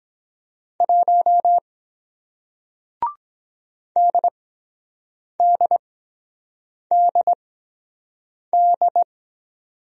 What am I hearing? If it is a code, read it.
Morse code, heard as a keyed tone, about 23 wpm, spaced as for 7 wpm, 700 Hz. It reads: 1 DDDD